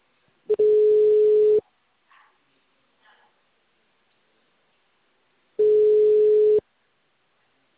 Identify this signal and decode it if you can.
Ringback tone (Europe, 425 Hz).